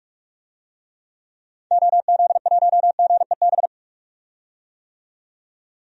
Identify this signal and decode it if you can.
Morse code, heard as a keyed tone, about 47 wpm, 700 Hz. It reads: OZ1GEB
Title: DXCC-100 Call Signs - 6 Characters - 45wpm